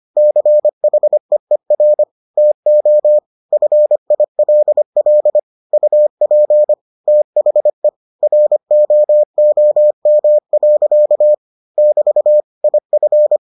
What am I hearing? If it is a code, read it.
Morse code, heard as a keyed tone, about 25 wpm, 600 Hz. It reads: CHEER TO FILL UP THE ROOM. <BT> IF